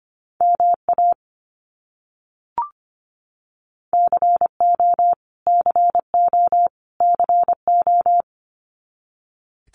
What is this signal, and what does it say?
Morse code, heard as a keyed tone, about 25 wpm, 700 Hz. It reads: MA CO CO CO
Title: US States Repeated 3x 25wpm